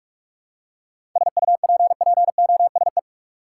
Morse code, heard as a keyed tone, about 44 wpm, 700 Hz. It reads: SUPPOSE
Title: Seven Letter Words - 45wpm